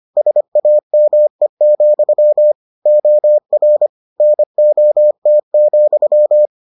Morse code, heard as a keyed tone, about 25 wpm, 600 Hz. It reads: SAME, OR NOT,